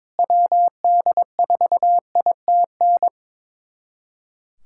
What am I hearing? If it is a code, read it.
Morse code, heard as a keyed tone, about 22 wpm, 700 Hz. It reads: WD4ITN
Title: USA Callsign DE Callsign 22wpm